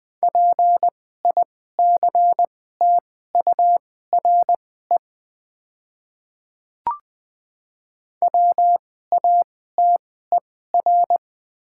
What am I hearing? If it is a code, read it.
Morse code, heard as a keyed tone, about 20 wpm, spaced as for 14 wpm, 700 Hz. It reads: PICTURE WATER
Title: 2x Character Spacing - Top 500 Words - 20wpm